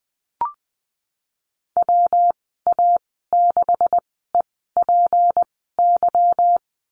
Morse code, heard as a keyed tone, about 20 wpm, spaced as for 15 wpm, 700 Hz. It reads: WA6EPY
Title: 154 Lesson 25g - Letter p - Call Signs 20wpm